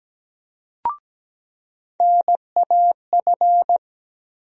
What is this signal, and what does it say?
Morse code, heard as a keyed tone, about 17 wpm, 700 Hz. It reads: NAF
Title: USA Call Sign Postfixes - 3 Letters 17wpm